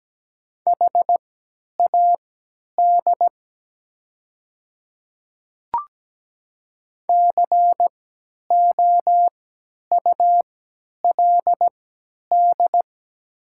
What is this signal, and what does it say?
Morse code, heard as a keyed tone, about 17 wpm, spaced as for 10 wpm, 700 Hz. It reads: HAD COULD
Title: 3x Character Spacing - Top 500 Words - 17wpm